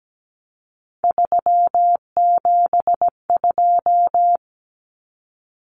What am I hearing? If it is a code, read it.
Morse code, heard as a keyed tone, about 17 wpm, 700 Hz. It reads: 372
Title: Common Cut Numbers - 3-Digit 17wpm